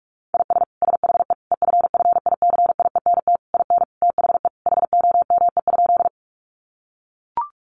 Morse code, heard as a keyed tone, about 44 wpm, 700 Hz. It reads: IS SHE EFFICIENT IN THE HOME?
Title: Sentences from Top 1300 Words 45wpm